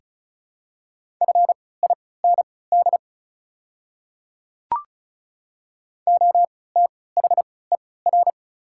Morse code, heard as a keyed tone, about 36 wpm, spaced as for 20 wpm, 690 Hz. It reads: FIND OTHER
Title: 3x Character Spacing - Top 100 Words - 35wpm